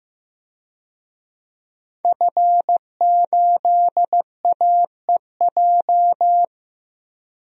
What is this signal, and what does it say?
Morse code, heard as a keyed tone, about 15 wpm, 700 Hz. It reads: F8AEJ
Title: DXCC-100 Call Signs - 5 Characters - 15wpm